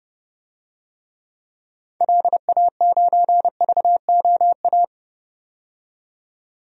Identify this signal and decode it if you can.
Morse code, heard as a keyed tone, about 30 wpm, 700 Hz. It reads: LA9VOA
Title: DXCC-100 Call Signs - 6 Characters - 30wpm